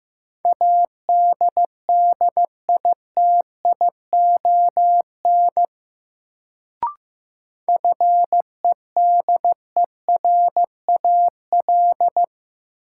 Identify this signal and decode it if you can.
Morse code, heard as a keyed tone, about 15 wpm, 700 Hz. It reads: ADDITION FEDERAL
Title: Sets of 3 Words - New Word Drill - 15wpm